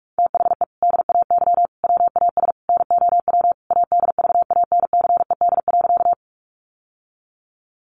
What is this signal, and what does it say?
Morse code, heard as a keyed tone, about 44 wpm, 700 Hz. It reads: THE DAY WAS NOW ADVANCED.
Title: Sentences from Top 1200 Words 45wpm